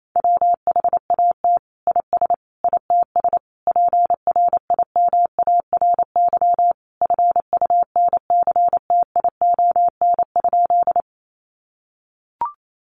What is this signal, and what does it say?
Morse code, heard as a keyed tone, about 28 wpm, 700 Hz. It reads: WHAT IS ITS PRIMARY FUNCTION?
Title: Sentences from Top 1100 Words 28wpm